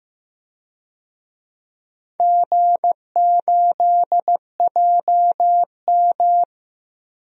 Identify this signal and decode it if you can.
Morse code, heard as a keyed tone, about 15 wpm, 700 Hz. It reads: G8JM
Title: DXCC-100 Call Signs - 4 Characters - 15wpm